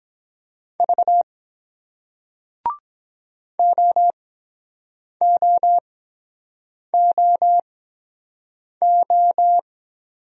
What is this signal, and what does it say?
Morse code, heard as a keyed tone, about 22 wpm, spaced as for 7 wpm, 700 Hz. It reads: V OOOO